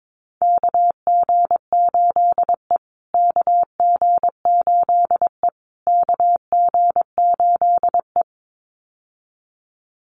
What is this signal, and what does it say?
Morse code, heard as a keyed tone, about 22 wpm, 700 Hz. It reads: KG8E KG8E KG8E
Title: USA Call Signs Repeated 3x 22wpm